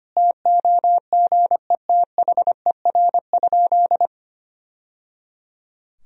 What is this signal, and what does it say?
Morse code, heard as a keyed tone, about 25 wpm, 700 Hz. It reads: TOGETHER?
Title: Sentences from Top 1000 Words - Review Part 2 - 25wpm